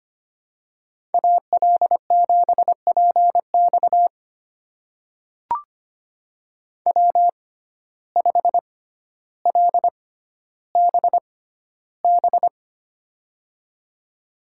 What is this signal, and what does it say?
Morse code, heard as a keyed tone, about 25 wpm, spaced as for 13 wpm, 700 Hz. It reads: AL7PX W5LBB
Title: USA Call Signs - 5 Characters - 6x Character Spacing - 25wpm